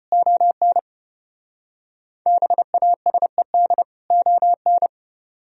Morse code, heard as a keyed tone, about 34 wpm, 700 Hz. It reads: ON BASED ON